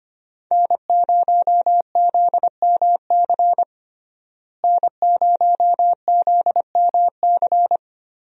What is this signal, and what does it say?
Morse code, heard as a keyed tone, about 25 wpm, 700 Hz. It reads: N0ZMC N0ZMC
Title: USA Call Signs - 5 Characters - Repeated 3x - 25wpm